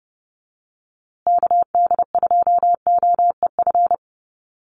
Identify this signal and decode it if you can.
Morse code, heard as a keyed tone, about 30 wpm, 700 Hz. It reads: KD2OEF
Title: USA Call Signs Repeated 3x 30wpm